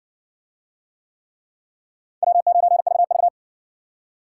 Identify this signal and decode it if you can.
Morse code, heard as a keyed tone, about 60 wpm, 700 Hz. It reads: W9VV